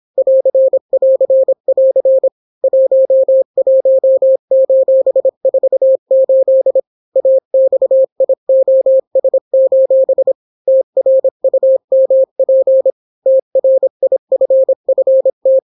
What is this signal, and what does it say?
Morse code, heard as a keyed tone, about 26 wpm, 530 Hz. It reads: <AR><AR><AR> 11:48 AXIOS: TRUMP TRIFFT